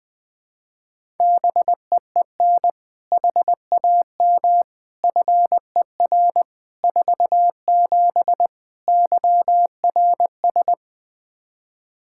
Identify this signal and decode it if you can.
Morse code, heard as a keyed tone, about 20 wpm, 700 Hz. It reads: BEEN HAM FER 47 YRS
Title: Practice QSO - Exchange 3 - More Info - 20wpm